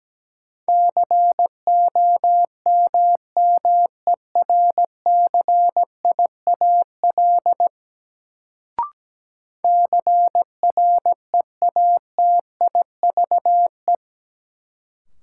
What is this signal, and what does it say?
Morse code, heard as a keyed tone, about 17 wpm, 700 Hz. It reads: COMMERCIAL CREATIVE
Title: Sentences from Top 1000 Words - Review Part 5 - New Word Drill - 17wpm